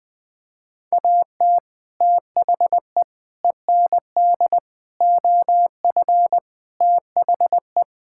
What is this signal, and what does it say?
Morse code, heard as a keyed tone, about 20 wpm, 700 Hz. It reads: AT THE END OF THE
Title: Sentences with 3-Word Phrases - Spoken Beginning - 20wpm